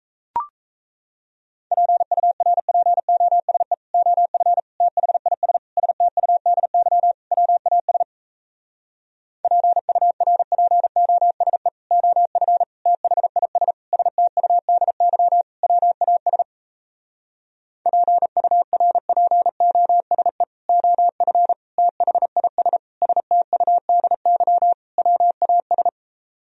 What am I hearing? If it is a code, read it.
Morse code, heard as a keyed tone, about 38 wpm, 700 Hz. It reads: PURPOSE OF THIS STUDY WAS PURPOSE OF THIS STUDY WAS PURPOSE OF THIS STUDY WAS